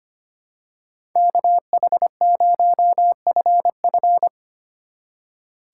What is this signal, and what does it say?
Morse code, heard as a keyed tone, about 24 wpm, 700 Hz. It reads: KH0FF
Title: Speed Racing - USA 2x2 Call Signs 25wpm